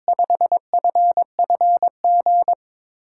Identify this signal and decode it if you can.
Morse code, heard as a keyed tone, about 22 wpm, 700 Hz. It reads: 5FFG
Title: USA Call Signs - 6 Characters - 8x Character Spacing - 22wpm